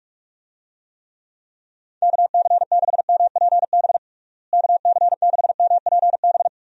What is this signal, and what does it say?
Morse code, heard as a keyed tone, about 42 wpm, 700 Hz. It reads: KC6MPB KC6MPB
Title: USA Call Signs - 6 Characters - Repeated 3x - 45wpm